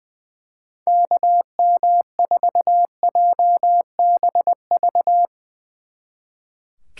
Morse code, heard as a keyed tone, about 22 wpm, 700 Hz. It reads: KM4JBV
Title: Speed Racing - USA 2x3 Call Signs 20wpm